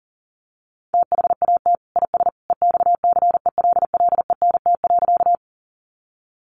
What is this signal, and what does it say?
Morse code, heard as a keyed tone, about 40 wpm, 700 Hz. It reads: THAT IS EXCELLENT.
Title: Sentences from Top 1000 Words 40wpm